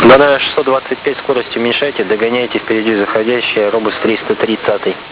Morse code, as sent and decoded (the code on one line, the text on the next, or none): none